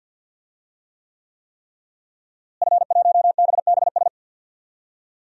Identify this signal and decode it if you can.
Morse code, heard as a keyed tone, about 51 wpm, 700 Hz. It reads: F1BBS
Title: DXCC-100 Call Signs - 5 Characters - Repeated 3x - 50wpm